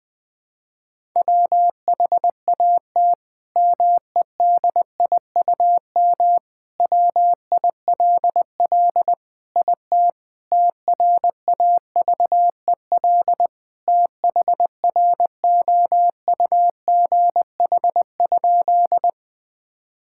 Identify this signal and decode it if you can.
Morse code, heard as a keyed tone, about 20 wpm, 700 Hz. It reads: WHAT MEDIUM WILL IT TRAVEL THROUGH?
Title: Sentences from Top 1000 Words - Review Part 6 - 20wpm